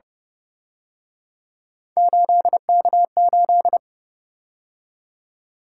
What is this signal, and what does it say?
Morse code, heard as a keyed tone, about 28 wpm, 700 Hz. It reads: 8K8